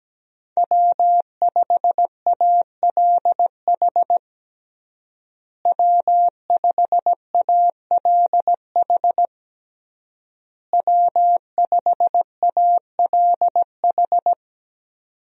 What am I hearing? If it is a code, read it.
Morse code, heard as a keyed tone, about 17 wpm, 700 Hz. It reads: W5ALH W5ALH W5ALH
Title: USA Call Signs - 5 Characters - Repeated 3x - 17wpm